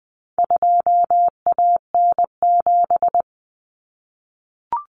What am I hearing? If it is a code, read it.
Morse code, heard as a keyed tone, about 20 wpm, 700 Hz. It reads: 2AN7
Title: Common Cut Numbers - 4-Digit 20wpm